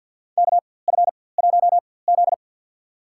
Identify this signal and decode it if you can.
Morse code, heard as a keyed tone, about 52 wpm, spaced as for 24 wpm, 700 Hz. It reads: KF1Z
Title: USA Call Signs - 4x Character Spacing - 50wpm